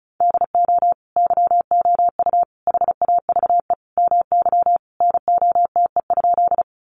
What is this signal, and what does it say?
Morse code, heard as a keyed tone, about 34 wpm, 700 Hz. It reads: DO YOU HAVE MY NOTE?
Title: Sentences from Top 500 Words 35wpm